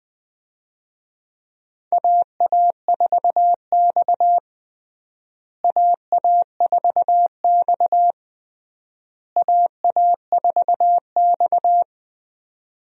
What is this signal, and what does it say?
Morse code, heard as a keyed tone, about 21 wpm, 690 Hz. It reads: AA4X AA4X AA4X
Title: USA Call Signs - 4 Characters - Repeated 3x - 20wpm